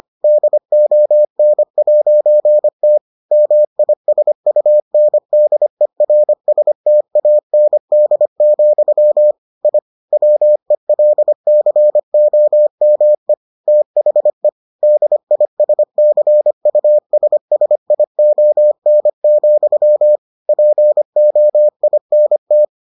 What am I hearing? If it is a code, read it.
Morse code, heard as a keyed tone, about 25 wpm, 600 Hz. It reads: DON'T MISUNDERSTAND, I WELCOME THE DISCUSSION, POINT